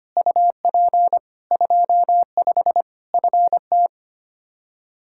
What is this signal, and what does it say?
Morse code, heard as a keyed tone, about 25 wpm, 700 Hz. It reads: UP 25 FT
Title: K1USN SST QSO Practice 25wpm